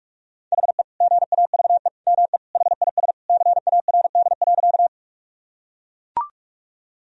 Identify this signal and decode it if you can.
Morse code, heard as a keyed tone, about 41 wpm, 700 Hz. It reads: HE GAVE ME HIS CARD.
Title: Sentences from Top 1000 Words - Review Part 4 - 45wpm